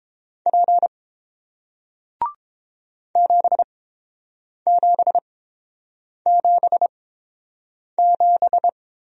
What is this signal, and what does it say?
Morse code, heard as a keyed tone, about 32 wpm, spaced as for 7 wpm, 700 Hz. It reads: P 7777